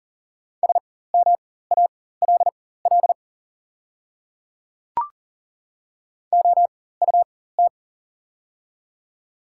Morse code, heard as a keyed tone, about 37 wpm, spaced as for 18 wpm, 700 Hz. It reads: SMALL OUT
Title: 4x Character Spacing - Top 200 Words - 40wpm